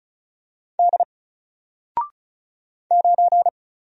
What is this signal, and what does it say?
Morse code, heard as a keyed tone, about 36 wpm, 700 Hz. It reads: D 9
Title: Single Letter-Number Alt 35wpm